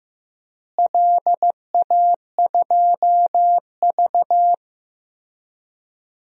Morse code, heard as a keyed tone, about 15 wpm, 700 Hz. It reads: LA2V
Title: DXCC-100 Call Signs - 4 Characters - Repeated 3x - 15wpm